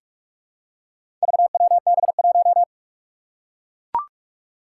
Morse code, heard as a keyed tone, about 44 wpm, 700 Hz. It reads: VWB1
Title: Four Challenging Letter-Number Combinations 45wpm